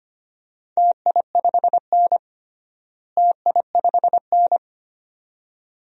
Morse code, heard as a keyed tone, about 25 wpm, 700 Hz. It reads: TI5N TI5N
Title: DXCC-100 Call Signs - 4 Characters - Repeated 2x - 25wpm